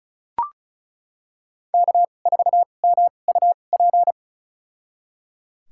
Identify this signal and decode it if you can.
Morse code, heard as a keyed tone, about 34 wpm, spaced as for 25 wpm, 700 Hz. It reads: K4MUP
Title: USA Call Signs - 2x Character Spacing - 35wpm